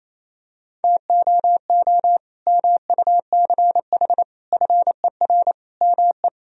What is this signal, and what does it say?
Morse code, heard as a keyed tone, about 27 wpm, 700 Hz. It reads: TOO MUCH FER ME
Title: Sentences from Top 200 Words and Abbreviations 28wpm